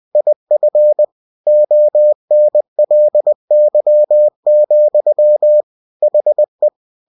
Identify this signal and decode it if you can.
Morse code, heard as a keyed tone, about 20 wpm, 600 Hz. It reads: IF ONLY, HE
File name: cw-books / winnie-the-pooh / winnie-20wpm-page-0090